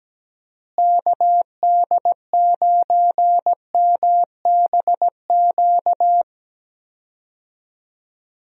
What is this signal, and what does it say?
Morse code, heard as a keyed tone, about 17 wpm, 700 Hz. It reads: KD9MBQ
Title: USA Call Signs - 6 Characters - Repeated 2x - 17wpm